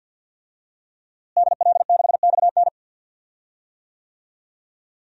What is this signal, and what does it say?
Morse code, heard as a keyed tone, about 51 wpm, 700 Hz. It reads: DL6XN
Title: DXCC-100 Call Signs - 5 Characters - Repeated 3x - 50wpm